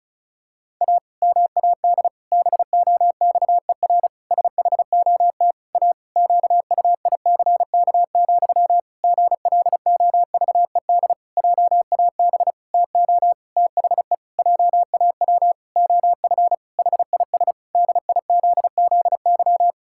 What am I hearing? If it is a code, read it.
Morse code, heard as a keyed tone, about 36 wpm, 700 Hz. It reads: A MAD BOXER SHOT A QUICK, GLOVED JAB TO THE JAW OF HIS DIZZY
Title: Pangram Challenge v2 - 35wpm